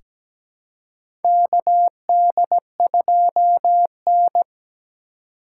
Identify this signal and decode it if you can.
Morse code, heard as a keyed tone, about 17 wpm, 700 Hz. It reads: KD2N
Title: USA Call Signs - 4 Characters - 4x Character Spacing - 17wpm